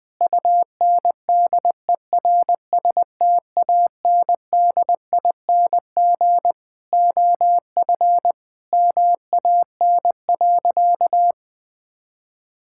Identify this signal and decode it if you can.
Morse code, heard as a keyed tone, about 20 wpm, 700 Hz. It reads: UNDERSTANDING OF MAN.
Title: Sentences from Top 1000 Words - Review Part 5 - 20wpm